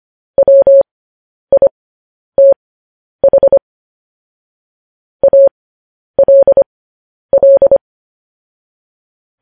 Morse code, heard as a keyed tone, about 25 wpm, spaced as for 10 wpm, 550 Hz. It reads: WITH ALL
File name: podcast / News headlines-1x-25x10